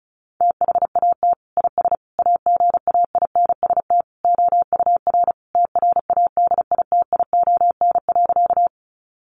Morse code, heard as a keyed tone, about 34 wpm, 700 Hz. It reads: THAT IS AGAINST OUR TRADITION.
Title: Sentences from Top 1500 Words 35wpm